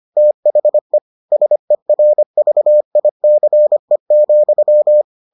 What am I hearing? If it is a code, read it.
Morse code, heard as a keyed tone, about 25 wpm, 600 Hz. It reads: THE SERVICE,